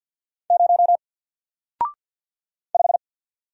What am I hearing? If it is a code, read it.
Morse code, heard as a keyed tone, about 48 wpm, 700 Hz. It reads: 0 5
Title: Single Letter-Number 50wpm